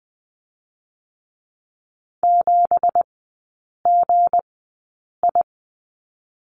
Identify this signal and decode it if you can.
Morse code, heard as a keyed tone, about 19 wpm, 700 Hz. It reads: 7 G I
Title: 178 Lesson 29e - Slash - Three Characters 20wpm